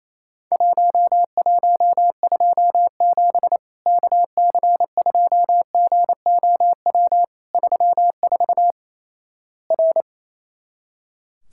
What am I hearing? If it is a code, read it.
Morse code, heard as a keyed tone, about 28 wpm, 700 Hz. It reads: 1127 KC2GOW 34 R
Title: Worked All Europe DX Contest - QTC Exchange - 28wpm